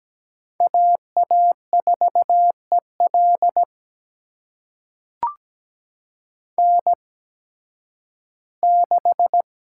Morse code, heard as a keyed tone, about 17 wpm, spaced as for 11 wpm, 700 Hz. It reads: AA4EL N6
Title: USA Call Signs - 5 Characters - 8x Character Spacing - 17wpm